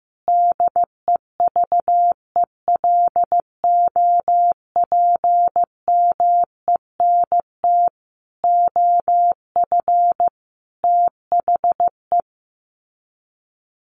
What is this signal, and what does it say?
Morse code, heard as a keyed tone, about 15 wpm, 700 Hz. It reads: DEVELOPMENT OF THE
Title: Sets of 5 Words Encore2 15wpm